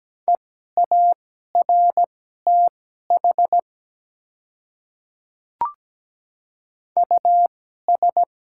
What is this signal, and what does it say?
Morse code, heard as a keyed tone, about 17 wpm, spaced as for 12 wpm, 700 Hz. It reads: EARTH US
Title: 2x Character Spacing - Top 200 Words - 17wpm